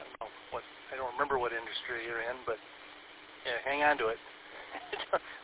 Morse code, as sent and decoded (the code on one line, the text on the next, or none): none